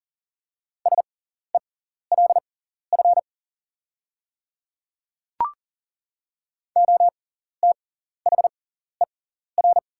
Morse code, heard as a keyed tone, about 40 wpm, spaced as for 14 wpm, 690 Hz. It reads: SELF OTHER
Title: 6x Character Spacing - Top 200 Words - 40wpm